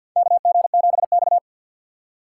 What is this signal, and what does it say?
Morse code, heard as a keyed tone, about 53 wpm, 700 Hz. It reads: KG7X
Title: USA Call Signs - 4 Characters - 6x Character Spacing - 50wpm